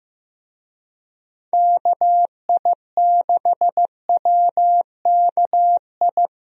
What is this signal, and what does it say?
Morse code, heard as a keyed tone, about 15 wpm, 700 Hz. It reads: KI6WKI
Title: USA Call Signs - 6 Characters - Repeated 3x - 15wpm